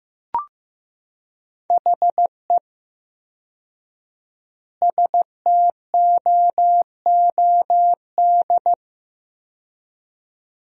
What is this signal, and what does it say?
Morse code, heard as a keyed tone, about 15 wpm, 700 Hz. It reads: HE STOOD
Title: 4x Word-Spacing - Sentences from Top 500 Words - 15wpm